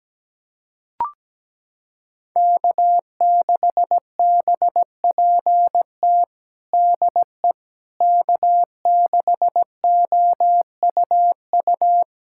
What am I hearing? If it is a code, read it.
Morse code, heard as a keyed tone, about 17 wpm, 700 Hz. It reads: K6BPT DE K6OUU
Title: USA Callsign DE Callsign - 5 Characters - 17wpm